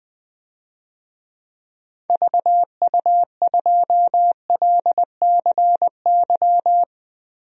Morse code, heard as a keyed tone, about 20 wpm, 690 Hz. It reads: VU2LCY
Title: DXCC-100 Call Signs - 6 Characters - Repeated 2x - 20wpm